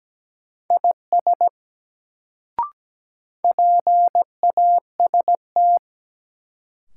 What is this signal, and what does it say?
Morse code, heard as a keyed tone, about 17 wpm, 700 Hz. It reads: IS PAST
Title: Sets of 2 Words - New Word Drill - 17wpm